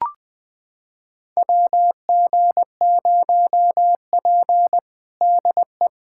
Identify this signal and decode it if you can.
Morse code, heard as a keyed tone, about 20 wpm, 700 Hz. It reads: WG0P DE
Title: USA Callsign DE Callsign - 4 Characters - 20wpm